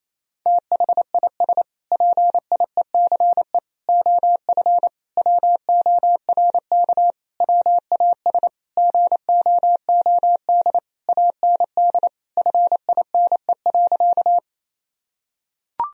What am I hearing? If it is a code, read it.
Morse code, heard as a keyed tone, about 27 wpm, 700 Hz. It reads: THIS PIECE OF WORK WAS GOOD AND FINE.
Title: Sentences from Top 1000 Words - Review Part 2 - 28wpm